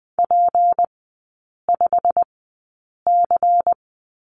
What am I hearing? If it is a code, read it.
Morse code, heard as a keyed tone, about 20 wpm, spaced as for 8 wpm, 700 Hz. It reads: P5C
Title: 198 Lesson 32e - Letter k - Three Characters 20wpm